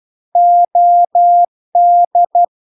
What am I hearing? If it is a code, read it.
Morse code, heard as a keyed tone, about 12 wpm, 700 Hz. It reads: OD